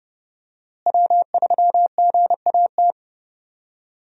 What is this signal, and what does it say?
Morse code, heard as a keyed tone, about 30 wpm, 700 Hz. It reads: W3GAT
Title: USA Call Signs - 5 Characters - Repeated 3x - 30wpm